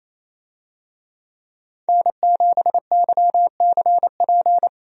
Morse code, heard as a keyed tone, about 28 wpm, 700 Hz. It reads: N7YCP
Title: USA Call Signs - 5 Characters - Repeated 2x - 28wpm